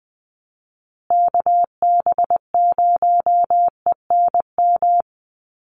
Morse code, heard as a keyed tone, about 20 wpm, 700 Hz. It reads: KB0ENM
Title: USA Call Signs Repeated 2x 20wpm